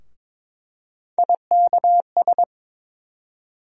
Morse code, heard as a keyed tone, about 21 wpm, 700 Hz. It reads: IKS
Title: USA Call Sign Postfixes - 3 Letters 22wpm